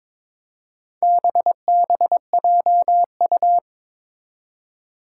Morse code, heard as a keyed tone, about 22 wpm, 700 Hz. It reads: BBJU
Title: Four Challenging Letter-Number Combinations 22wpm